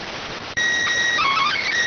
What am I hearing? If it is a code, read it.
SSTV (slow-scan television), PD 290.